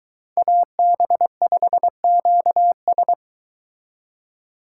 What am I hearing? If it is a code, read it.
Morse code, heard as a keyed tone, about 22 wpm, 700 Hz. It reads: AB5QS